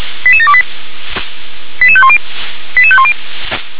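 Five-tone selective call (ZVEI), heard as quick runs of five tones, tones of about 70 ms. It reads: 88728 88510 88510
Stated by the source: ZVEI five-tone selective call received on an SDR.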